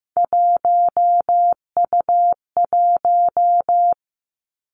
Morse code, heard as a keyed tone, about 15 wpm, 700 Hz. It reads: 1U1
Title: Three Challenging Letter-Number Combinations 15wpm